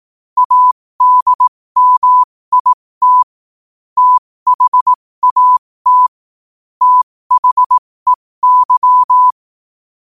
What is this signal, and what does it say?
Morse code, heard as a keyed tone, about 18 wpm, spaced as for 15 wpm, 1000 Hz. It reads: ADMIT THAT THEY
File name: qotdfiles / 15wpm